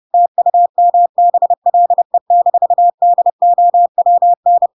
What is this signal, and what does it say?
Morse code, heard as a keyed tone, about 30 wpm, 700 Hz. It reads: TUMBLE-DOWN
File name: Pinocchio / Pinocchio ch3 30 30 ik1hge